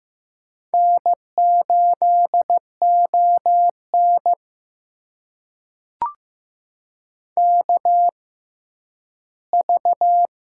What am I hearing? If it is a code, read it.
Morse code, heard as a keyed tone, about 15 wpm, spaced as for 10 wpm, 700 Hz. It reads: N8ON KV
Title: USA Call Signs - 4 Characters - 6x Character Spacing - 15wpm